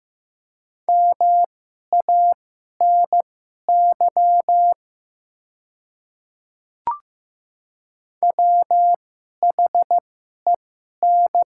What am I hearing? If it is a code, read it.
Morse code, heard as a keyed tone, about 15 wpm, spaced as for 11 wpm, 700 Hz. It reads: MANY WHEN
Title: 2x Character Spacing - Top 200 Words - 15wpm